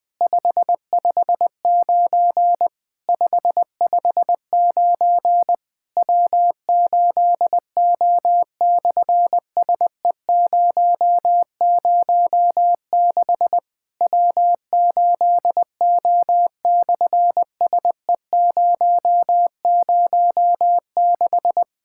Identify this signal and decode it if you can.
Morse code, heard as a keyed tone, about 20 wpm, 700 Hz. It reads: 559 559 W8O/SE006 W8O/SE006
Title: SOTA QSO Practice 20wpm